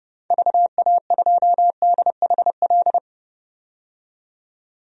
Morse code, heard as a keyed tone, about 32 wpm, 700 Hz. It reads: VA2DHL
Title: DXCC-100 Call Signs - 6 Characters - 30wpm